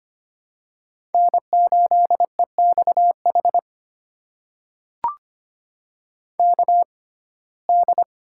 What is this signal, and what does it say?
Morse code, heard as a keyed tone, about 25 wpm, spaced as for 18 wpm, 700 Hz. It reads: N8EXH KD
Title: USA Call Signs - 5 Characters - 6x Character Spacing - 25wpm